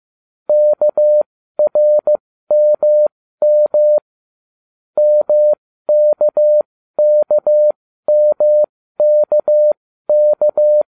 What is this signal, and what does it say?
Morse code, heard as a keyed tone, about 15 wpm, spaced as for 13 wpm, 600 Hz. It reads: KRMM MKKMKK